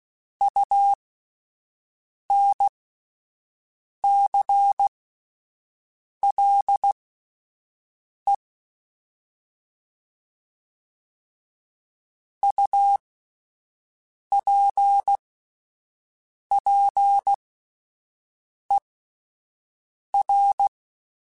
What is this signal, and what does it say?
Morse code, heard as a keyed tone, about 16 wpm, spaced as for 5 wpm, 800 Hz. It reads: UNCLE UPPER
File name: Alphabetical / Alphabetical mp3s / Us @5